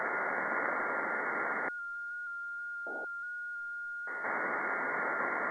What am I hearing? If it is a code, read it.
FT8.
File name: contests / 2019-june=vhf-k5tr- -0101 0-20